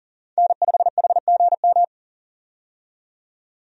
Morse code, heard as a keyed tone, about 40 wpm, 700 Hz. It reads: N5HGM
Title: USA Call Signs - 5 Characters - 40wpm